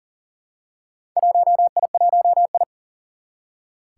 Morse code, heard as a keyed tone, about 38 wpm, 700 Hz. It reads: 1I1I
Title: Four Challenging Letter-Number Combinations 40wpm